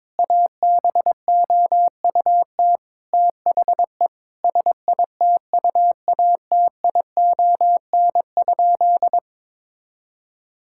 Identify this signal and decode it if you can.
Morse code, heard as a keyed tone, about 22 wpm, 700 Hz. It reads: ABOUT THE SITUATION?
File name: practice / 22wpm / Sentences with 8-Word Phrases - 22wpm